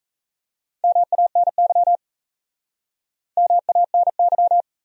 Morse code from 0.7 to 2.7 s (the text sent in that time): MANY